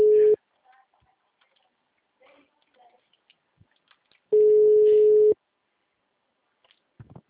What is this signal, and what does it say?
Ringback tone (Europe, 425 Hz), with speech.